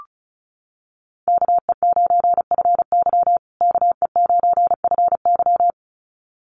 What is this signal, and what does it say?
Morse code, heard as a keyed tone, about 34 wpm, 700 Hz. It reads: KE9FY KE9FY
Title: USA Call Signs Repeated 2x 35wpm